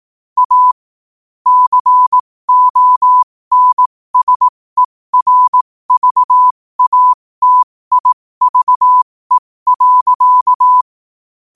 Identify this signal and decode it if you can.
Morse code, heard as a keyed tone, about 18 wpm, spaced as for 15 wpm, 1000 Hz. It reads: A CONSERVATIVE.
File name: qotdfiles / 15wpm